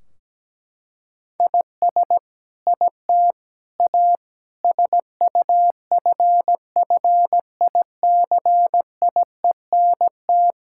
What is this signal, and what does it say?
Morse code, heard as a keyed tone, about 17 wpm, 700 Hz. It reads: IS IT A SUFFICIENT
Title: Sentences from Top 1600 Words 17wpm